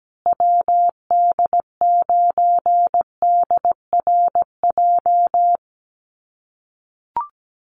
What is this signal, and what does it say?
Morse code, heard as a keyed tone, about 17 wpm, 700 Hz. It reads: WD9DRJ
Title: USA Call Signs Repeated 3x 17wpm